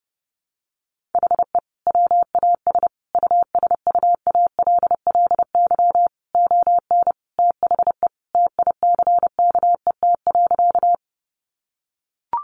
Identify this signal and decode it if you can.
Morse code, heard as a keyed tone, about 30 wpm, 700 Hz. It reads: HE WAS USUALLY ON THE TICKET.
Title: Sentences from Top 1400 Words 30wpm